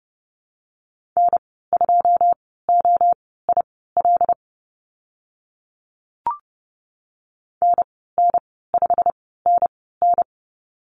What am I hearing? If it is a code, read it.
Morse code, heard as a keyed tone, about 30 wpm, spaced as for 17 wpm, 690 Hz. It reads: N2OIL NN5NN
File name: Lesson 15g - Number 5 - Call Signs 30wpm